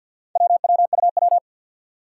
Morse code, heard as a keyed tone, about 50 wpm, 700 Hz. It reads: WWUW
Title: Four Challenging Letter-Number Combinations 50wpm